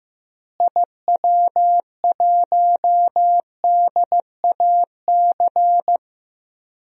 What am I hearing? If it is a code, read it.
Morse code, heard as a keyed tone, about 16 wpm, 700 Hz. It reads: IW1DAC
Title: DXCC-100 Call Signs - 6 Characters - Repeated 2x - 15wpm